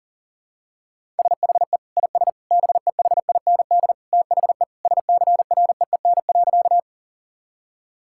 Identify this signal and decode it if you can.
Morse code, heard as a keyed tone, about 37 wpm, 700 Hz. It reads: SHE IS BEHIND THE SCREEN.
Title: Sentences from Top 1000 Words - Review Part 6 - 40wpm